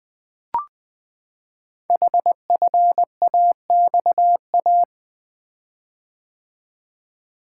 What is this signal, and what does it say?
Morse code, heard as a keyed tone, about 20 wpm, 700 Hz. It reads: HFAXA
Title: Code Groups - 5 Letters - 20wpm